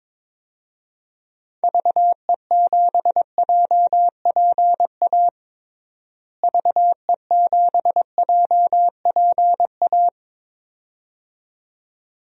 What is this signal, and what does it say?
Morse code, heard as a keyed tone, about 21 wpm, 700 Hz. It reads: VE7JPA VE7JPA
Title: DXCC-100 Call Signs - 6 Characters - Repeated 2x - 22wpm